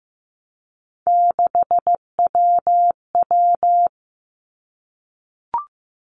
Morse code, heard as a keyed tone, about 15 wpm, 700 Hz. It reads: 6WW
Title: Three Challenging Letter-Number Combinations 15wpm